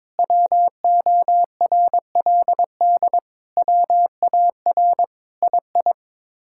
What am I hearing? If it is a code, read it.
Morse code, heard as a keyed tone, about 22 wpm, 700 Hz. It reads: WORLD WAR II
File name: practice / 22wpm / Sets of 7 Words - 22wpm